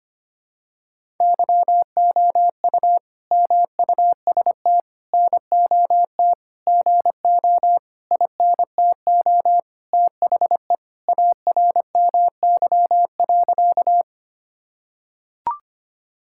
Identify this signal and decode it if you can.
Morse code, heard as a keyed tone, about 25 wpm, 700 Hz. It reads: YOU MUST NOT GO INTO THE ARMY.
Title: Sentences from Top 1000 Words - Review Part 5 - 25wpm